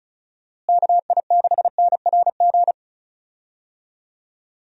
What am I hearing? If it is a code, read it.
Morse code, heard as a keyed tone, about 34 wpm, 700 Hz. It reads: KI6NRG
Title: USA Call Signs - 6 Characters - Repeated 3x - 35wpm